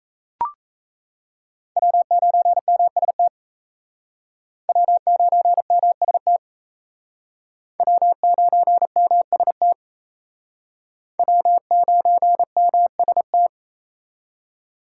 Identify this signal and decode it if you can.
Morse code, heard as a keyed tone, about 35 wpm, 700 Hz. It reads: W9MST W9MST W9MST W9MST
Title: Speed Racing - USA 1x3 Call Signs 28wpm